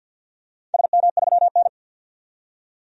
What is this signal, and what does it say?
Morse code, heard as a keyed tone, about 52 wpm, 700 Hz. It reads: SM3N
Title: DXCC-100 Call Signs - 4 Characters - 50wpm